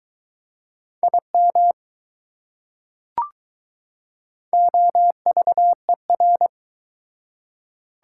Morse code, heard as a keyed tone, about 22 wpm, 690 Hz. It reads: IM OVER